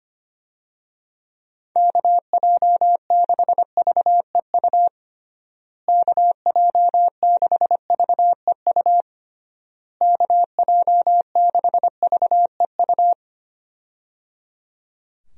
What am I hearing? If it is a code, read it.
Morse code, heard as a keyed tone, about 25 wpm, 700 Hz. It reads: KJ6VEU KJ6VEU KJ6VEU
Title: USA Call Signs - 6 Characters - Repeated 3x - 25wpm